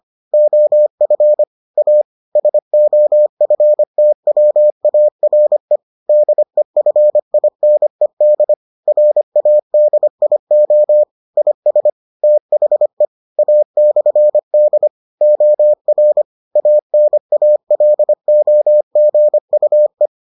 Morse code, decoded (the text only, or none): OF A SOFTWARE DEFINED RADIO IS THE A/D OR ANALOGUE